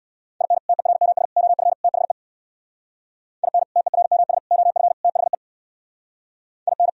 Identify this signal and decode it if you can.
Morse code, heard as a keyed tone, about 64 wpm, 700 Hz. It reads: IN TERMS OF THE IN TERMS OF THE IN